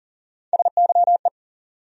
Morse code, heard as a keyed tone, about 40 wpm, 700 Hz. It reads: SYE
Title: USA Call Sign Postfixes - 3 Letters 40wpm